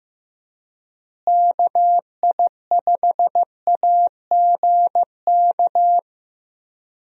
Morse code, heard as a keyed tone, about 15 wpm, 700 Hz. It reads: KI5AGK
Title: USA Call Signs - 6 Characters - Repeated 2x - 15wpm